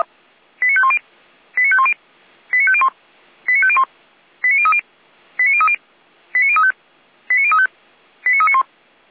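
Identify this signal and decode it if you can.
Five-tone selective call (ZVEI), heard as quick runs of five tones, tones of about 70 ms. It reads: 89510 89510 89501 89501 89030 89030 89035 89035 89391
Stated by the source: ZVEI five-tone selective call received on SDR.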